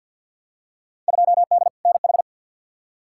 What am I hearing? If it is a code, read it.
Morse code, heard as a keyed tone, about 52 wpm, 700 Hz. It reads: 2D NH
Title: Field Day Exchanges Encore - 50wpm